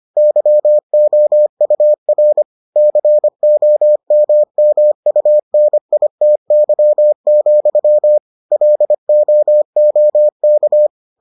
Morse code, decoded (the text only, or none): YOUR COMMUNITY, LOOK